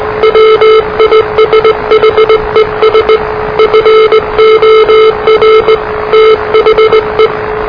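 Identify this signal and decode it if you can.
Morse code, heard as a keyed tone, about 19 wpm, 440 Hz, with noise. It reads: WISHES FOR THE
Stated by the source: Morse challenge from NZ Net, the national amateur radio net of Aotearoa/New Zealand, which meets on 3535.0 kHz.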